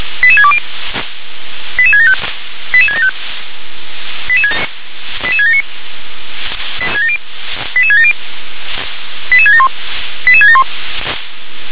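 Five-tone selective call (ZVEI), heard as quick runs of five tones, tones of about 70 ms. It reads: 88620 88675 88675 88679 88670 88671 88671
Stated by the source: ZVEI five-tone selective call received with an SDR.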